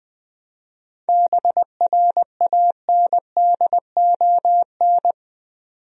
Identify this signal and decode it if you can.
Morse code, heard as a keyed tone, about 20 wpm, 700 Hz. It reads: BRANDON
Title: USA Common Names - 20wpm